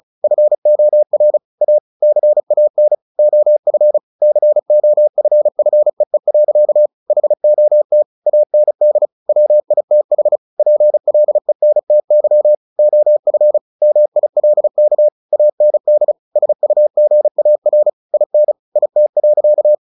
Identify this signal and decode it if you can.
Morse code, heard as a keyed tone, about 34 wpm, 600 Hz. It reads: FOR A CAN OF COFFEE. HOT AND WITH PLENTY OF MILK AND SUGAR IN IT.
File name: the old man and the sea 2 35wpm